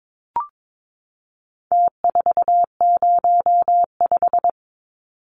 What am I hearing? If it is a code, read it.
Morse code, heard as a keyed tone, about 22 wpm, 700 Hz. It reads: T405
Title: Numbers - Contest Style - 4-Digit 22wpm